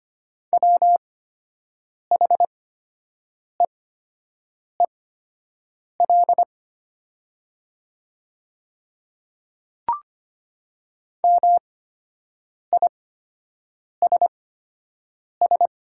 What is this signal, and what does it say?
Morse code, heard as a keyed tone, about 24 wpm, spaced as for 7 wpm, 700 Hz. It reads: WHEEL MISS
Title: 8x Character Spacing - Top 500 Words - 25wpm